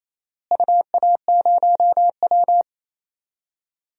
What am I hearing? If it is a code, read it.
Morse code, heard as a keyed tone, about 28 wpm, 700 Hz. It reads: UA0W
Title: DXCC-100 Call Signs - 4 Characters - Repeated 3x - 28wpm